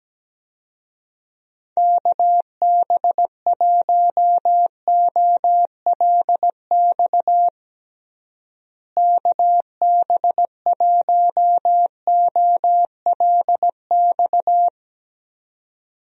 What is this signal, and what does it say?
Morse code, heard as a keyed tone, about 17 wpm, 700 Hz. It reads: KB1OLX KB1OLX
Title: USA Call Signs - 6 Characters - Repeated 2x - 17wpm